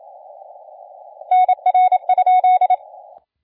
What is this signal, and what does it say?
Morse code, heard as a keyed tone, about 27 wpm, 740 Hz. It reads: NR?
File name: la4c / sac-cw-2017 / CONTEST AUDIO 1100Z